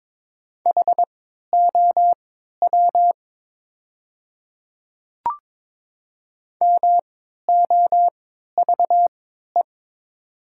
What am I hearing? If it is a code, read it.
Morse code, heard as a keyed tone, about 22 wpm, spaced as for 12 wpm, 700 Hz. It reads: HOW MOVE
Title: 3x Character Spacing - Top 200 Words - 22wpm